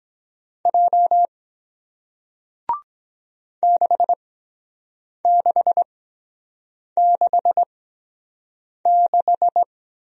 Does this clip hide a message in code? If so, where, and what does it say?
Yes: Morse code from 0.7 to 9.6 s, heard as a keyed tone, about 23 wpm, spaced as for 7 wpm, 700 Hz. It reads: J 6666